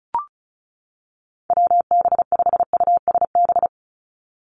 Morse code, heard as a keyed tone, about 34 wpm, 700 Hz. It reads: WB5USB
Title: NA SOTA Activator Callsigns 35wpm